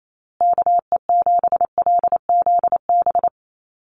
Morse code, heard as a keyed tone, about 28 wpm, 700 Hz. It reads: KE7LZB
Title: USA Call Signs Repeated 2x 28wpm